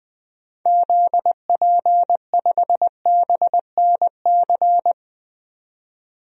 Morse code, heard as a keyed tone, about 20 wpm, 700 Hz. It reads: ZP5BNC
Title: DXCC-100 Call Signs - 6 Characters - 20wpm